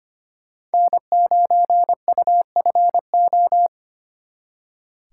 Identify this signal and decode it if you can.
Morse code, heard as a keyed tone, about 32 wpm, 700 Hz. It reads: N9UFO